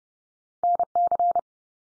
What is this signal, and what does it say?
Morse code, heard as a keyed tone, about 30 wpm, 700 Hz. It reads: NC
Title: US States 30WPM